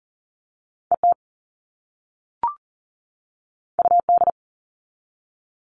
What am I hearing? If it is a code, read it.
Morse code, heard as a keyed tone, about 40 wpm, 700 Hz. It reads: ET UD
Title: All Cut Numbers - 2-Digit 40wpm